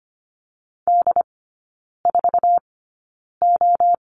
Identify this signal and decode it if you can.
Morse code, heard as a keyed tone, about 25 wpm, spaced as for 9 wpm, 700 Hz. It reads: D4O
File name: Lesson 15e - Number 5 - Three Characters 25wpm